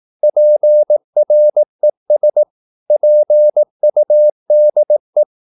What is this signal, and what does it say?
Morse code, heard as a keyed tone, about 18 wpm, 600 Hz. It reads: PRES PUDE
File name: Korte ord DK 18wpm